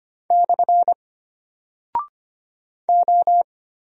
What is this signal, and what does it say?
Morse code, heard as a keyed tone, about 25 wpm, 700 Hz. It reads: / O
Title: Single Character (Letter:Number:Punctuation:Prosign) Alt 25wpm